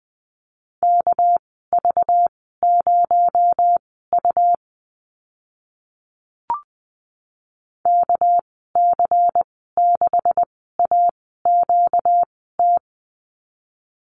Morse code, heard as a keyed tone, about 20 wpm, spaced as for 14 wpm, 700 Hz. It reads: KV0U KC6AQT
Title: 245 Lesson 39g - Letter z - Call Signs 20wpm